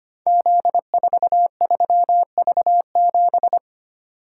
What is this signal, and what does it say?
Morse code, heard as a keyed tone, about 25 wpm, 700 Hz. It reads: Z43V7
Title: Code Groups - 5 Letter:Numbers - 25wpm